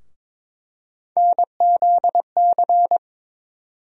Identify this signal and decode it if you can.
Morse code, heard as a keyed tone, about 22 wpm, 700 Hz. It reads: NZC
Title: USA Call Sign Postfixes - 3 Letters 22wpm